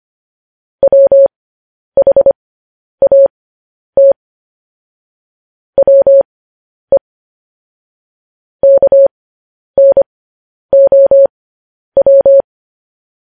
Morse code, heard as a keyed tone, about 25 wpm, spaced as for 10 wpm, 550 Hz. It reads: WHAT WE KNOW
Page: news headlines in morse code at 10 wpm